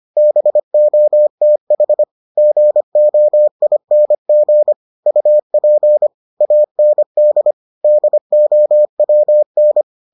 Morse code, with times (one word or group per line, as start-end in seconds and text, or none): 0.2-2.0 BOTH
2.4-4.7 GOING
5.1-6.1 UP
6.4-7.5 AND
7.8-9.8 DOWN